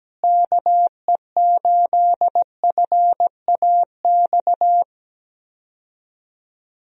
Morse code, heard as a keyed tone, about 17 wpm, 700 Hz. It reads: KE8FAX
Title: USA Call Signs - 6 Characters - Repeated 3x - 17wpm